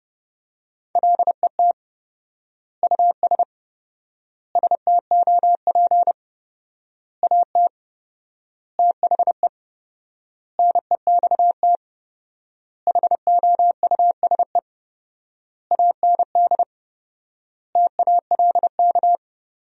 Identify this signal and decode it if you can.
Morse code, heard as a keyed tone, about 28 wpm, 700 Hz. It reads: LET US STOP AT THE NEXT HOUSE AND TALK
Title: 4x Word-Spacing - Sentences from Top 400 Words - 30wpm